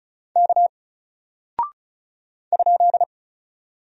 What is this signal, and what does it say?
Morse code, heard as a keyed tone, about 36 wpm, 700 Hz. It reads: K ?
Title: Single Character (Letter:Number:Punctuation:Prosign) 35wpm